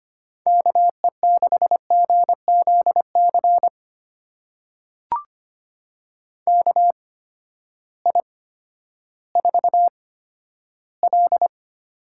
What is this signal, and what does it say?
Morse code, heard as a keyed tone, about 25 wpm, spaced as for 12 wpm, 700 Hz. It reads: KE6GZC KI4L
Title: USA Call Signs - 6 Characters - 8x Character Spacing - 25wpm